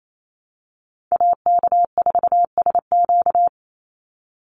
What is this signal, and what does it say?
Morse code, heard as a keyed tone, about 28 wpm, 700 Hz. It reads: AK4SQ
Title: Call Signs Repeated 2x - 28wpm